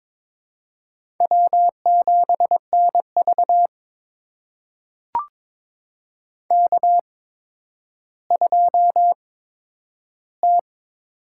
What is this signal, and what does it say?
Morse code, heard as a keyed tone, about 22 wpm, spaced as for 11 wpm, 700 Hz. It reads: W7NV K2T
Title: USA Call Signs - 4 Characters - 8x Character Spacing - 22wpm